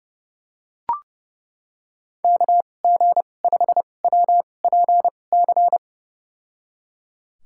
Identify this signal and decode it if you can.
Morse code, heard as a keyed tone, about 30 wpm, spaced as for 22 wpm, 700 Hz. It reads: KG5WPC
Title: USA Call Signs - 2x Character Spacing - 30wpm